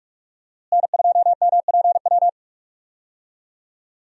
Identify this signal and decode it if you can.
Morse code, heard as a keyed tone, about 46 wpm, 700 Hz. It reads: N2MPW
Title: USA Call Signs - 5 Characters - Repeated 3x - 45wpm